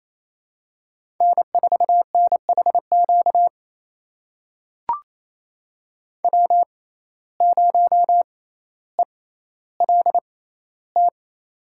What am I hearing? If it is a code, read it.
Morse code, heard as a keyed tone, about 27 wpm, spaced as for 14 wpm, 700 Hz. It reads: N4NHQ W0ELT
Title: USA Call Signs - 5 Characters - 6x Character Spacing - 28wpm